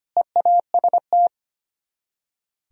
Morse code, heard as a keyed tone, about 32 wpm, 700 Hz. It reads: EAST